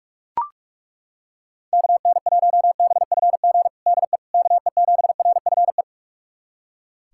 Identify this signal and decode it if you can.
Morse code, heard as a keyed tone, about 41 wpm, 700 Hz. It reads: KN1BFG DE KE7RFE
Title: USA Callsign DE Callsign - 6 Characters - 45wpm